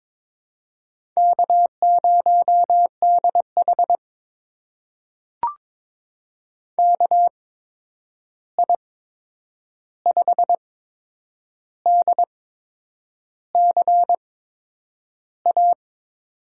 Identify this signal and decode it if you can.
Morse code, heard as a keyed tone, about 21 wpm, spaced as for 8 wpm, 700 Hz. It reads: K0DH KI5DCA
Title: USA Call Signs - 8x Character Spacing - 22wpm